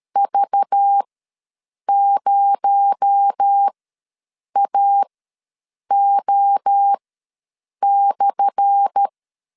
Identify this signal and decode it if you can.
Morse code, heard as a keyed tone, about 13 wpm, spaced as for 7 wpm, 800 Hz. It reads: V0AO/
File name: LEZIONI / 10 SIGNS / 10D COMA ws 12wpm 3618